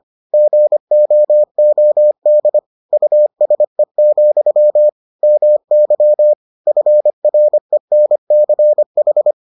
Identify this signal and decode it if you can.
Morse code, heard as a keyed tone, about 25 wpm, 600 Hz. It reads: GOOD USE, MY FRENCH